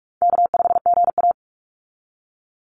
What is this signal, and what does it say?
Morse code, heard as a keyed tone, about 44 wpm, 700 Hz. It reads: K5GA
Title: Call Signs - 45wpm